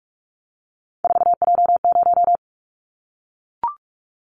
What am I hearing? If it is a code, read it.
Morse code, heard as a keyed tone, about 44 wpm, 700 Hz. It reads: 4J0
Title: Callsign Prefixes for Top 100 DXCC - 45wpm